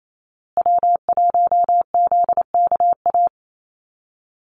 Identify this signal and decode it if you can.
Morse code, heard as a keyed tone, about 28 wpm, 700 Hz. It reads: W1ZKA
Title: USA Call Signs Repeated 3x 28wpm